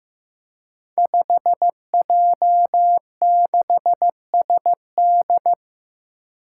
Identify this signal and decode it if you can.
Morse code, heard as a keyed tone, about 15 wpm, 700 Hz. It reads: 5J6SD
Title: Five Challenging Letter-Number Combinations 15wpm